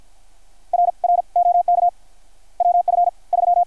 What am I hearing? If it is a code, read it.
Morse code, heard as a keyed tone, about 53 wpm, spaced as for 40 wpm, 700 Hz. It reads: R R OK WF3
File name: archived / CRSnet